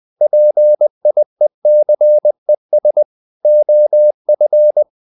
Morse code, heard as a keyed tone, about 20 wpm, 600 Hz. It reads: PIECES OF
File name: the old man and the sea part 2 20wpm